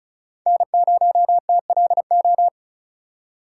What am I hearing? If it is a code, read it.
Morse code, heard as a keyed tone, about 34 wpm, 700 Hz. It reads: N0TLO
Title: USA Call Signs - 2x Character Spacing - 35wpm